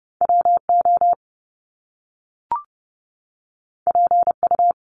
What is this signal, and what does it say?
Morse code, heard as a keyed tone, about 30 wpm, 700 Hz. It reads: WO PU
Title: Top Two Letter Combinations in Words 30wpm